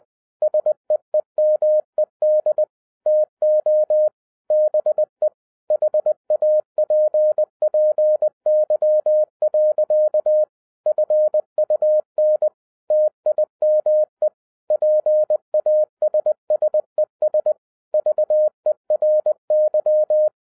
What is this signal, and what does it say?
Morse code, heard as a keyed tone, about 20 wpm, 610 Hz. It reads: SEEMED TO BE HAPPY. FUN TIME PASSES VERY